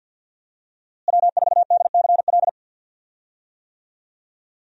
Morse code, heard as a keyed tone, about 53 wpm, 700 Hz. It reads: W4DCL
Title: USA Call Signs - 5 Characters - Repeated 2x - 50wpm